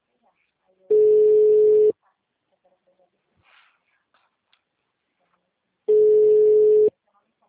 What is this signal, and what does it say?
Ringback tone (Europe, 425 Hz), with speech.